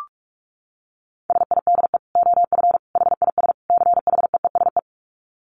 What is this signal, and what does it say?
Morse code, heard as a keyed tone, about 44 wpm, 700 Hz. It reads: SIDE OF HIS CHEESE
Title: Dit Challenge 45wpm